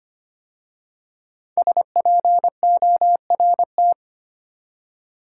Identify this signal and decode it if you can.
Morse code, heard as a keyed tone, about 25 wpm, 700 Hz. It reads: SPORT
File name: 5 Letter Words - First 3 Letters Given - 25wpm